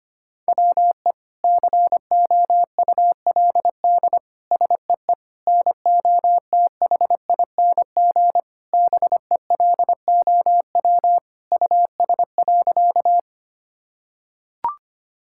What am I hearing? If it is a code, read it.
Morse code, heard as a keyed tone, about 25 wpm, 700 Hz. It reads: WE COULD SEE NOTHING BELOW US.
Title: Sentences from Top 1000 Words - Review Part 5 - 25wpm